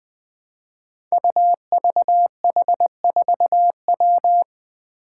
Morse code, heard as a keyed tone, about 20 wpm, 700 Hz. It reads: UVH4W
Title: Five Challenging Letter-Number Combinations 20wpm